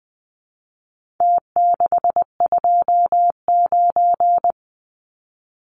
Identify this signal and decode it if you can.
Morse code, heard as a keyed tone, about 20 wpm, 700 Hz. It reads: T629
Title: Numbers - Contest Style - 4-Digit 20wpm